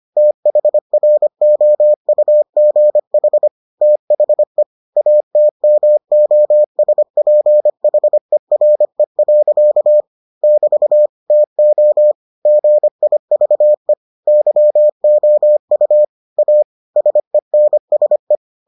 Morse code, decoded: THROUGH THE ATMOSPHERE. <BT> TO GIVE YOU A SENSE

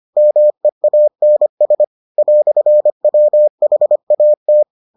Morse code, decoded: MEANS "WHAT

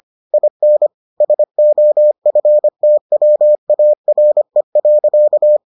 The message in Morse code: IN SOFTWARE.

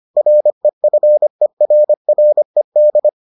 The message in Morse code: REFERRED

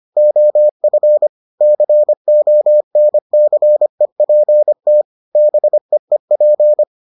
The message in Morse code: OF CONCEPT BEEP

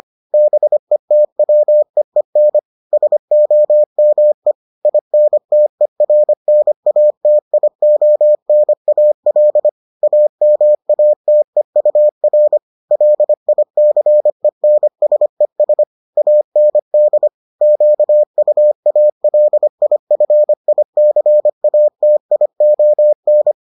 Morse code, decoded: BETWEEN SOME INTERNATIONAL AMATEUR LICENSES AND QUALIFICATION

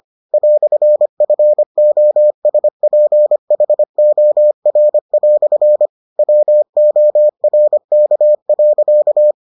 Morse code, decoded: "FOSPHOR" WORK.

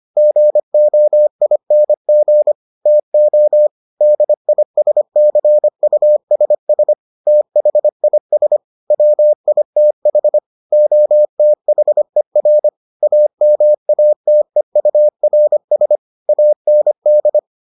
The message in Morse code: GOING TO DISCUSS THIS WITH OTHER AMATEURS AND

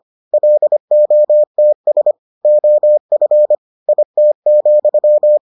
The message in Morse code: LOTS OF IT,

